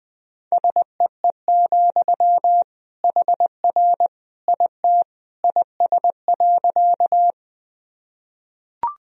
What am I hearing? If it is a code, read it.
Morse code, heard as a keyed tone, about 20 wpm, 700 Hz. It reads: SEE, HR IT IS.
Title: Sentences from Top 200 Words and Abbreviations 20wpm